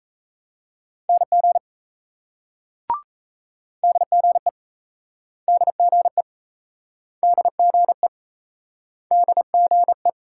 Morse code, heard as a keyed tone, about 38 wpm, 700 Hz. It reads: NG DGE DGE DGE DGE